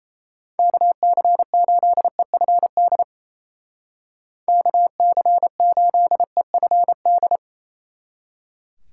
Morse code, heard as a keyed tone, about 35 wpm, 700 Hz. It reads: KC8EFD KC8EFD
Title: Speed Racing - USA Call Signs 28wpm